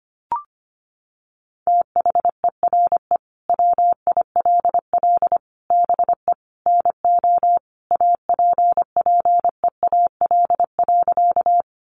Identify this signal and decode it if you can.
Morse code, heard as a keyed tone, about 25 wpm, 700 Hz. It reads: THERE WILL BE NO APPEAL.
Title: Sentences from Top 1300 Words 25wpm